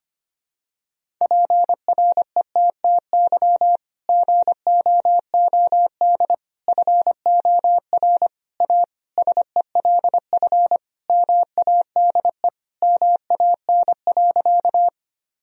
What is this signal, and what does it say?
Morse code, heard as a keyed tone, about 25 wpm, 700 Hz. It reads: PRETTY GOOD FOR A SELF MADE MAN.
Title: Sentences from Top 1000 Words - Review Part 1 - 25wpm